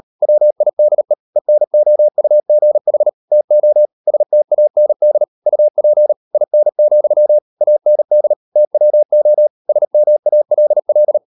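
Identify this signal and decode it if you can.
Morse code, heard as a keyed tone, about 38 wpm, 600 Hz. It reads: WIDE ENOUGH TO STAND UP IN, AND TWO SMALL
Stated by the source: Morse code practice at 38 WPM, sending Ken Follett's novel Never in English, from CW med Gnister.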